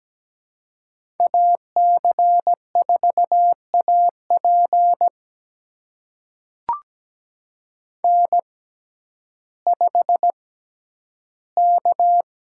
Morse code, heard as a keyed tone, about 17 wpm, spaced as for 10 wpm, 700 Hz. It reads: AC4AP N5K
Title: USA Call Signs - 5 Characters - 6x Character Spacing - 17wpm